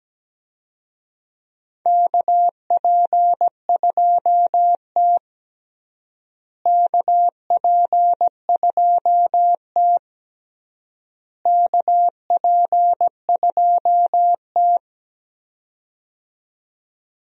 Morse code, heard as a keyed tone, about 17 wpm, 700 Hz. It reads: KP2T KP2T KP2T
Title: USA Call Signs - 4 Characters - Repeated 3x - 17wpm